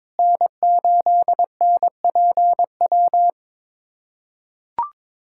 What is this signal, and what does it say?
Morse code, heard as a keyed tone, about 22 wpm, 700 Hz. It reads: N8NPW
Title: USA Callsign DE Callsign - 5 Characters - 22wpm